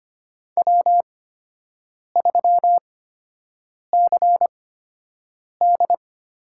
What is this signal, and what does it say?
Morse code, heard as a keyed tone, about 25 wpm, spaced as for 7 wpm, 700 Hz. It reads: W3CD
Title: USA Call Signs - 8x Character Spacing - 25wpm